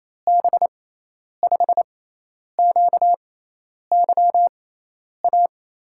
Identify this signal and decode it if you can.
Morse code, heard as a keyed tone, about 27 wpm, spaced as for 10 wpm, 700 Hz. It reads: B5QYA
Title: USA Call Signs - 6 Characters - 6x Character Spacing - 28wpm